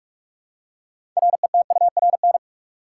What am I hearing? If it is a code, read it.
Morse code, heard as a keyed tone, about 46 wpm, 700 Hz. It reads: RETURN
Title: Six Letter Words - 45wpm